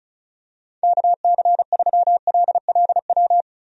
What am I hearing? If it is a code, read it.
Morse code, heard as a keyed tone, about 36 wpm, 700 Hz. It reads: KC3LLW
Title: USA Call Signs - 6 Characters - Repeated 3x - 35wpm